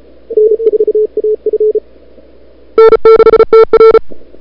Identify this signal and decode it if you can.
Morse code, heard as a keyed tone, about 36 wpm, 440 Hz. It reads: N4AF N6TR
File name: n6tr / Fab2007CWSprintHour1